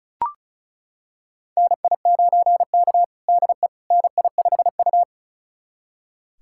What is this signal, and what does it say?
Morse code, heard as a keyed tone, about 36 wpm, 700 Hz. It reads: NI9K DE NI5U
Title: USA Callsign DE Callsign - 4 Characters - 35wpm